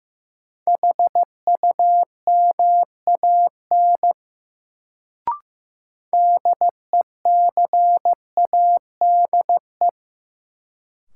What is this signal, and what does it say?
Morse code, heard as a keyed tone, about 15 wpm, 700 Hz. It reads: HUMAN DECADE
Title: Sets of 5 Words - New Word Drill - 15wpm